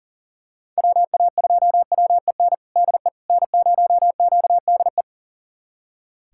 Morse code, heard as a keyed tone, about 39 wpm, 700 Hz. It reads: WA2WEN DE N0QDE
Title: USA Callsign DE Callsign 40wpm